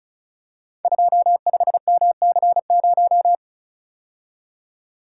Morse code, heard as a keyed tone, about 36 wpm, 700 Hz. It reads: 25MC0
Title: Code Groups - 5 Letter:Numbers - 35wpm